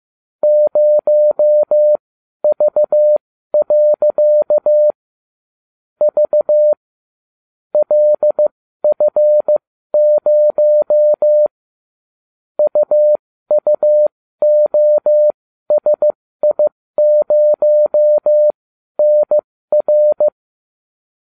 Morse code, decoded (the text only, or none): 0V. V LF0 UUOSI0NR